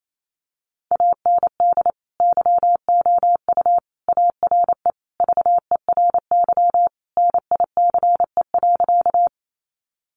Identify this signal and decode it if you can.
Morse code, heard as a keyed tone, about 28 wpm, 700 Hz. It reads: AND YOU ARE VERY NICE.
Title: Sentences from Top 1100 Words 28wpm